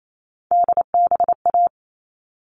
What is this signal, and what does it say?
Morse code, heard as a keyed tone, about 28 wpm, 700 Hz. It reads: DBA
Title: All Cut Numbers - 3-Digit 28wpm